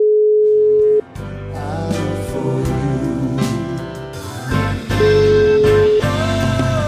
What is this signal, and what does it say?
Ringback tone (Europe, 425 Hz), with noise.